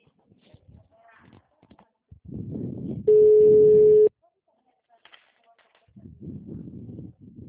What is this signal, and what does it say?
Ringback tone (Europe, 425 Hz), with speech.